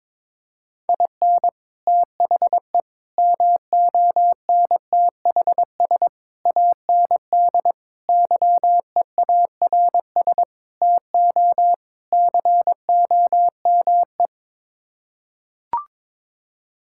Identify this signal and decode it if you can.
Morse code, heard as a keyed tone, about 22 wpm, 700 Hz. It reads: IN THE MONTHS AND YEARS TO COME
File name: practice / 22wpm / Sets of 7 Words - 22wpm